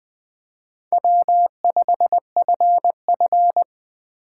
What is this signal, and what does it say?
Morse code, heard as a keyed tone, about 19 wpm, 700 Hz. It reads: W5FF